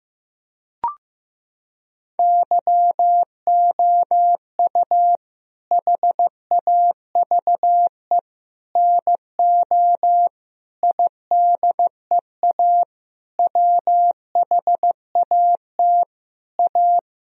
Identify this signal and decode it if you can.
Morse code, heard as a keyed tone, about 15 wpm, 700 Hz. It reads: YOU HAVE NO IDEA WHAT A
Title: Sentences from Top 1000 Words - Review Part 1 - 15wpm